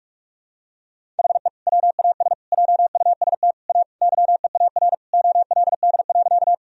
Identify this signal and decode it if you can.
Morse code, heard as a keyed tone, about 46 wpm, 700 Hz. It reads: HE WAS JUST A YEAR OLD.
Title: Sentences from Top 1000 Words - Review Part 1 - 45wpm